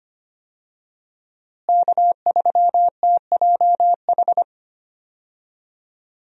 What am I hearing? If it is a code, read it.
Morse code, heard as a keyed tone, about 25 wpm, 700 Hz. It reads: K3TJH
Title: USA Call Signs - 5 Characters - Repeated 3x - 25wpm